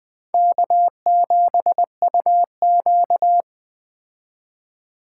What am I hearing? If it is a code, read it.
Morse code, heard as a keyed tone, about 21 wpm, 700 Hz. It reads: K7UQ
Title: Speed Racing - USA Call Signs 17wpm